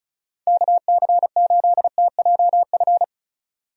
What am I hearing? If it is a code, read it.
Morse code, heard as a keyed tone, about 36 wpm, 700 Hz. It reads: KC8TJF
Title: USA Call Signs - 6 Characters - 35wpm